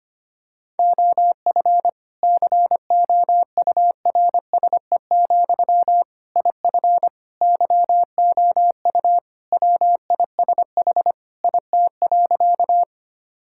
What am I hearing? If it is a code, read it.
Morse code, heard as a keyed tone, about 25 wpm, 700 Hz. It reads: OF COURSE, IF YOU WISH IT.
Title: Sentences from Top 1600 Words 25wpm